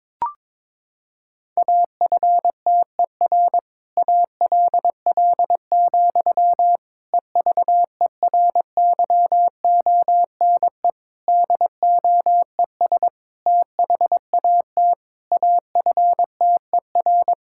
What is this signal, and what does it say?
Morse code, heard as a keyed tone, about 22 wpm, 700 Hz. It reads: AFTER ALL, EVERYONE DOES THAT AFTER
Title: Sentences from Top 1000 Words - Review Part 5 - 22wpm